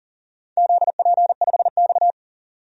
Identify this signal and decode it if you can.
Morse code, heard as a keyed tone, about 40 wpm, 700 Hz. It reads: ZP5X
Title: DXCC-100 Call Signs - 4 Characters - 40wpm